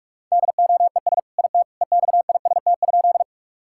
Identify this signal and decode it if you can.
Morse code, heard as a keyed tone, about 46 wpm, 700 Hz. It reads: DOES IT EXIST?
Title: Sentences from Top 1000 Words - Review Part 5 - 45wpm